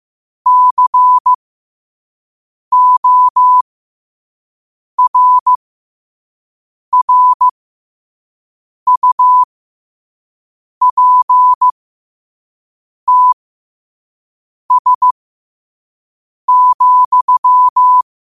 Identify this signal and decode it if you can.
Morse code, heard as a keyed tone, about 15 wpm, spaced as for 5 wpm, 1000 Hz. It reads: CORRUPTS,